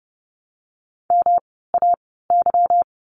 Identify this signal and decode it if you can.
Morse code, heard as a keyed tone, about 30 wpm, spaced as for 17 wpm, 700 Hz. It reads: MAY
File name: course / 30:10wpm / 230 Lesson 37f - Letter x - Words 30wpm